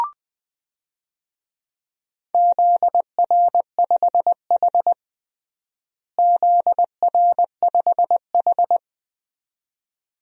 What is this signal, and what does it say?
Morse code, heard as a keyed tone, about 20 wpm, 700 Hz. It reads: ZR5H ZR5H
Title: DXCC-100 Call Signs - 4 Characters - Repeated 2x - 20wpm